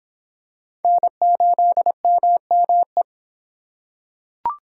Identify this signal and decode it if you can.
Morse code, heard as a keyed tone, about 25 wpm, 700 Hz. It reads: N8MME